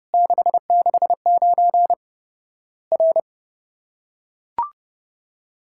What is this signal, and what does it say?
Morse code, heard as a keyed tone, about 30 wpm, 700 Hz. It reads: 669 R
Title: Worked All Europe DX Contest - QTC Exchange - 30wpm